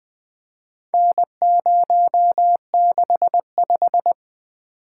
Morse code, heard as a keyed tone, about 20 wpm, 700 Hz. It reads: N065
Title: Four Letter-Number Combinations - 20wpm